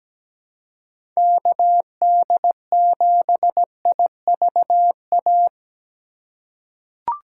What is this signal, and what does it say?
Morse code, heard as a keyed tone, about 17 wpm, 700 Hz. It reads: KD7IVA
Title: USA Call Signs - 6 Characters - 2x Character Spacing - 17wpm